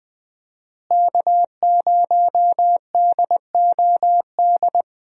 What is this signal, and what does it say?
Morse code, heard as a keyed tone, about 20 wpm, 700 Hz. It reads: K0DOD
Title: USA Call Signs - 5 Characters - Repeated 3x - 20wpm